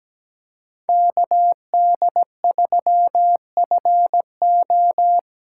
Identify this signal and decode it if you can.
Morse code, heard as a keyed tone, about 20 wpm, 700 Hz. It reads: KD3FO
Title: Speed Racing - USA 2x2 Call Signs 17wpm